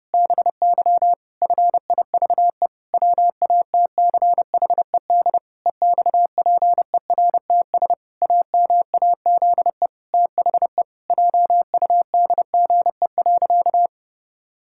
Morse code, heard as a keyed tone, about 30 wpm, 700 Hz. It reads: BY FIVE WATCHED EXPERTS AMAZE THE JUDGE.
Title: Pangram Challenge v2 - 30wpm